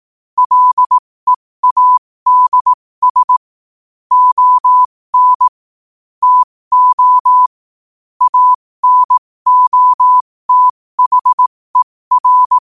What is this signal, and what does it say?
Morse code, heard as a keyed tone, about 18 wpm, spaced as for 15 wpm, 1000 Hz. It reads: LEADS ON TO ANOTHER
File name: qotdfiles / 15wpm